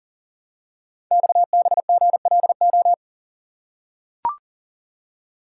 Morse code, heard as a keyed tone, about 38 wpm, 700 Hz. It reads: XBGLO
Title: Code Groups - 5 Letter:Numbers - 40wpm